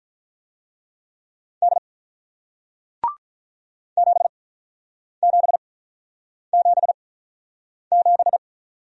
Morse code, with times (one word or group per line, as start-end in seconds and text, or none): 1.6-1.8 D
4.0-8.4 7777